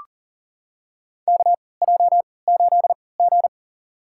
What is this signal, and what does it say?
Morse code, heard as a keyed tone, about 40 wpm, spaced as for 23 wpm, 700 Hz. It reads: KJ8G
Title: USA Call Signs - 4 Characters - 3x Character Spacing - 40wpm